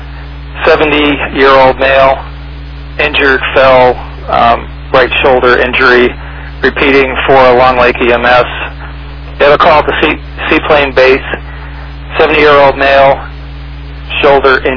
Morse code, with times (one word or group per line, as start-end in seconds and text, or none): none